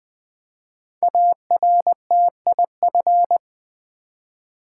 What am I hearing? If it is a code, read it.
Morse code, heard as a keyed tone, about 20 wpm, 700 Hz. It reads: ARTIF
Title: Build A Word - 5 Letters At A Time - 10-Letter Words - 20wpm